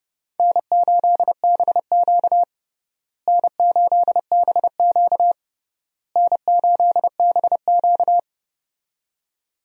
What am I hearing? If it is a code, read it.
Morse code, heard as a keyed tone, about 30 wpm, 700 Hz. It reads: N8BQ N8BQ N8BQ
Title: USA Call Signs - 4 Characters - Repeated 3x - 30wpm